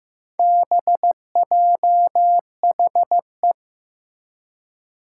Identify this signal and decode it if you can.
Morse code, heard as a keyed tone, about 15 wpm, 700 Hz. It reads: BJHE
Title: Four Challenging Letter-Number Combinations 15wpm